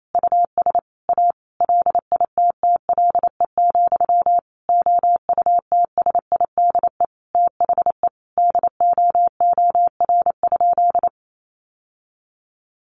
Morse code, heard as a keyed tone, about 28 wpm, 690 Hz. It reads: US A LITTLE, OUTSIDE THE DOOR?
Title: Sentences from Top 900 Words 28wpm